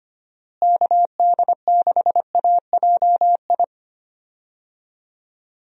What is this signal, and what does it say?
Morse code, heard as a keyed tone, about 25 wpm, 700 Hz. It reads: KD6AJI
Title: USA Call Signs - 6 Characters - Repeated 2x - 25wpm